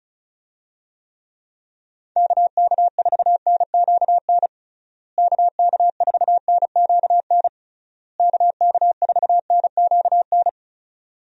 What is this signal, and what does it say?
Morse code, heard as a keyed tone, about 34 wpm, 700 Hz. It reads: KK4NQN KK4NQN KK4NQN
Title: USA Call Signs - 6 Characters - Repeated 3x - 35wpm